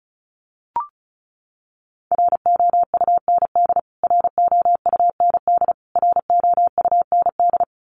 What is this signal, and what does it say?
Morse code, heard as a keyed tone, about 34 wpm, 700 Hz. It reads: ROUND ROUND ROUND
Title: Top 300 Words Repeated 35wpm